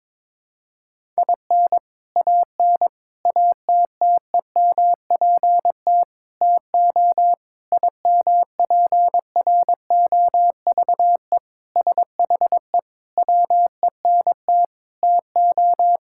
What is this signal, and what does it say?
Morse code, heard as a keyed tone, about 22 wpm, 700 Hz. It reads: IN AN ATTEMPT TO IMPROVE SHE WENT TO
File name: practice / 22wpm / Sentences with 4-Word Phrases - 22wpm